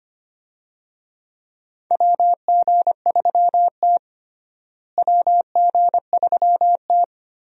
Morse code, heard as a keyed tone, about 25 wpm, 700 Hz. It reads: WG3T WG3T
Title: USA Call Signs - 4 Characters - Repeated 3x - 25wpm